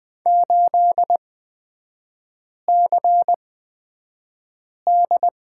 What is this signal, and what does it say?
Morse code, heard as a keyed tone, about 20 wpm, spaced as for 5 wpm, 700 Hz. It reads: 8CD
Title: 250 Lesson 40e - BK - Three Characters 20wpm